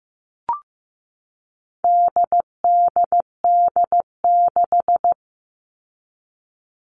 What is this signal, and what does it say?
Morse code, heard as a keyed tone, about 15 wpm, 700 Hz. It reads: DDD6
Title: All Cut Numbers - 4-Digit 15wpm